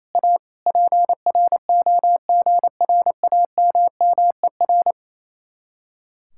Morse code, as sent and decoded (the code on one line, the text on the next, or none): .- / .--. .-. --- --. .-. .- -- -- . .-.
A PROGRAMMER